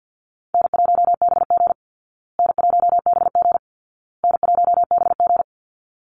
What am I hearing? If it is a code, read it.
Morse code, heard as a keyed tone, about 50 wpm, 700 Hz. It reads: N1BG N1BG N1BG
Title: Call Signs Repeated 2x - 50wpm